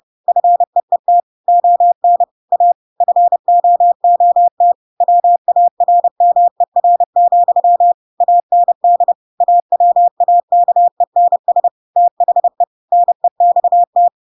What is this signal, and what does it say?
Morse code, heard as a keyed tone, about 30 wpm, 700 Hz. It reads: FEET ON A FOOT WARMER, AND AWAKENS THE NEXT
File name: Pinocchio / Pinocchio ch6 30 30 ik1hge